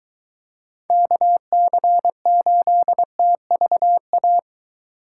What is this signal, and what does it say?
Morse code, heard as a keyed tone, about 22 wpm, 700 Hz. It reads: KC8TVA